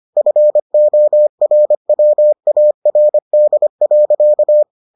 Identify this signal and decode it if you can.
Morse code, heard as a keyed tone, about 25 wpm, 600 Hz. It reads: FORWARD.